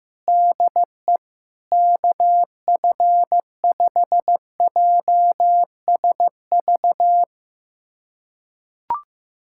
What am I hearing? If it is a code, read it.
Morse code, heard as a keyed tone, about 15 wpm, 700 Hz. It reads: DE KF5JSV
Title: USA Callsign DE Callsign - 6 Characters - 15wpm